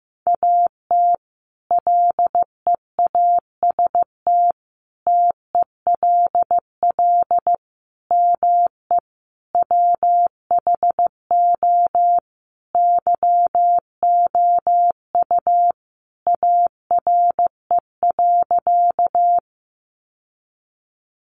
Morse code, heard as a keyed tone, about 15 wpm, 700 Hz. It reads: AT LEAST TELL ME WHO YOU ARE.
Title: Sentences from Top 700 Words 15wpm